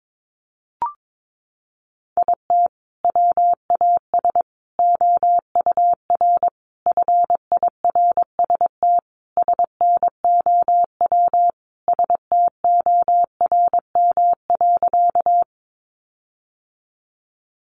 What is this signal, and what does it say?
Morse code, heard as a keyed tone, about 22 wpm, 700 Hz. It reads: IT WAS OUR FIRST SNOW STORM.
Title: Sentences from Top 1500 Words 22wpm